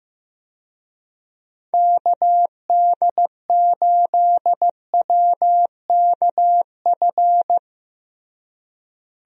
Morse code, heard as a keyed tone, about 15 wpm, 700 Hz. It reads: KD8WKF
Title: USA Call Signs - 6 Characters - Repeated 3x - 15wpm